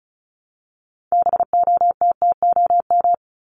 Morse code, heard as a keyed tone, about 34 wpm, 700 Hz. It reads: BOTTOM
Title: Top 751-1000 Words 35wpm